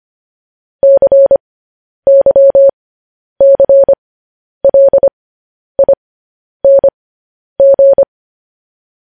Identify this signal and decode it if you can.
Morse code, heard as a keyed tone, about 25 wpm, spaced as for 10 wpm, 550 Hz. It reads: CYCLING
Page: news headlines in morse code at 10 wpm